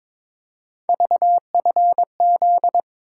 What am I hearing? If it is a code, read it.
Morse code, heard as a keyed tone, about 26 wpm, 700 Hz. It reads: VFZ
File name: Speed-Racing - Three Letter-Number Combinations - 22wpm